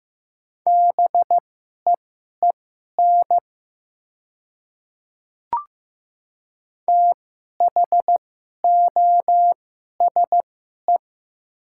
Morse code, heard as a keyed tone, about 15 wpm, spaced as for 11 wpm, 700 Hz. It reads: BEEN THOSE
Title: 2x Character Spacing - Top 300 Words - 15wpm